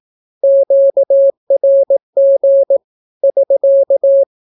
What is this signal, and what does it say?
Morse code, heard as a keyed tone, about 18 wpm, 550 Hz. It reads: QRG <SK>
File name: naukapodstaw / skroty / skroty